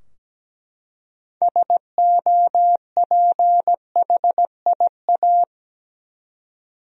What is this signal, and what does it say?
Morse code, heard as a keyed tone, about 17 wpm, 700 Hz. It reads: SOPHIA
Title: USA Common Names - 17wpm